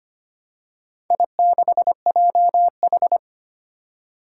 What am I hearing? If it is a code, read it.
Morse code, heard as a keyed tone, about 25 wpm, 700 Hz. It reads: I6JH
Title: Four Challenging Letter-Number Combinations 25wpm